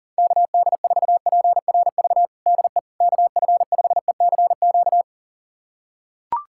Morse code, heard as a keyed tone, about 40 wpm, 700 Hz. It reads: KD4PRV DE KF5ECQ
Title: USA Callsign DE Callsign - 6 Characters - 40wpm